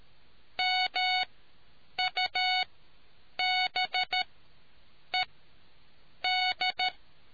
Morse code, heard as a keyed tone, about 13 wpm, spaced as for 7 wpm, 740 Hz. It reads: MUBED